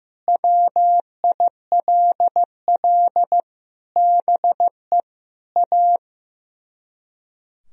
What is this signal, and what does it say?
Morse code, heard as a keyed tone, about 15 wpm, 700 Hz. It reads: WILL BE A
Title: Speed-Racing - Sets of 4 Words 15wpm